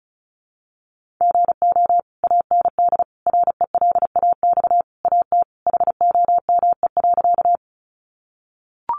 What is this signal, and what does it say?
Morse code, heard as a keyed tone, about 34 wpm, 700 Hz. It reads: GO AND RELAX AT HOME.
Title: Sentences from Top 1400 Words 35wpm